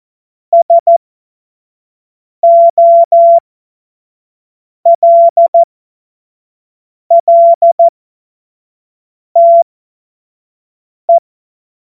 Morse code, heard as a keyed tone, about 14 wpm, spaced as for 5 wpm, 680 Hz. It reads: SOLLTE
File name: Morsekurs / Witze / Witz 03. August 2021, 25 F70